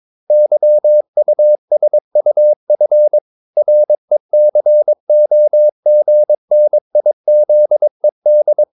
Morse code, heard as a keyed tone, about 22 wpm, 600 Hz. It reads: YUSUF RECOGNIZED
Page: Romanen Never af Ken Follet på engelsk.